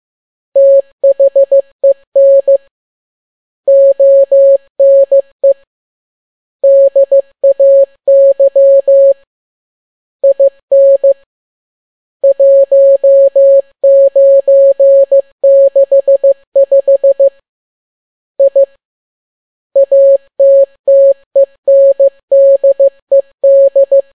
Morse code, heard as a keyed tone, about 15 wpm, 550 Hz. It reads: THEN ONE DAY IN 1965 I ATTENDED
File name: n6tt p1 15C13